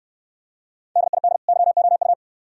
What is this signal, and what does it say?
Morse code, heard as a keyed tone, about 63 wpm, 690 Hz. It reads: DID YOU